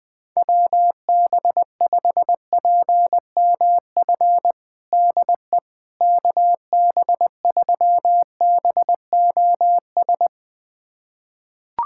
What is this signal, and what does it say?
Morse code, heard as a keyed tone, about 20 wpm, 700 Hz. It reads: WB5PMF DE KB3BOS
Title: USA Callsign DE Callsign 20wpm